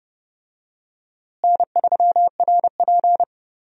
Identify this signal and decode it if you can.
Morse code, heard as a keyed tone, about 30 wpm, 700 Hz. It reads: N3RP
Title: USA Call Signs - 4 Characters - Repeated 2x - 30wpm